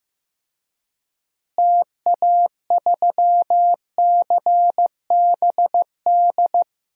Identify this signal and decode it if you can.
Morse code, heard as a keyed tone, about 15 wpm, 700 Hz. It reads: TA3CBD
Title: DXCC-100 Call Signs - 6 Characters - Repeated 3x - 15wpm